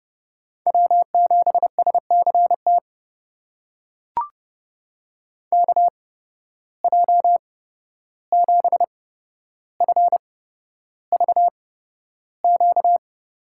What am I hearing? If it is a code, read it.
Morse code, heard as a keyed tone, about 30 wpm, spaced as for 12 wpm, 700 Hz. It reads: W7SCT KJ7FVQ
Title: USA Call Signs - 8x Character Spacing - 30wpm